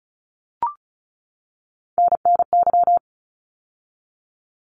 Morse code, heard as a keyed tone, about 34 wpm, 700 Hz. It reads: NNY
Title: ARRL Sections 35wpm